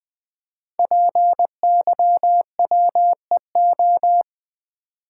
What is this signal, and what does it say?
Morse code, heard as a keyed tone, about 20 wpm, 700 Hz. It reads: PYWEO
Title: Code Groups - 5 Letters - 20wpm